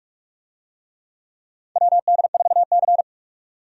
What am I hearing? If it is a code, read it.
Morse code, heard as a keyed tone, about 46 wpm, 700 Hz. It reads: WD4C
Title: USA Call Signs - 4 Characters - 45wpm